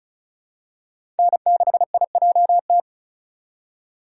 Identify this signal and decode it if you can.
Morse code, heard as a keyed tone, about 34 wpm, 700 Hz. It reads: N6IJT
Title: USA Call Signs - 5 Characters - 2x Character Spacing - 35wpm